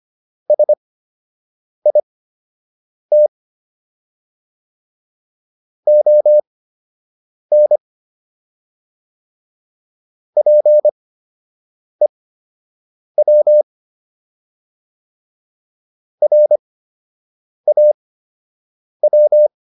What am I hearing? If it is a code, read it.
Morse code, heard as a keyed tone, about 25 wpm, spaced as for 7 wpm, 600 Hz. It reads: SIT ON PEW RAW